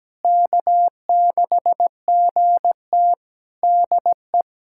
Morse code, heard as a keyed tone, about 17 wpm, 700 Hz. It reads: K6GT DE
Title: Practice QSO - Exchange 4 - Ending - 17wpm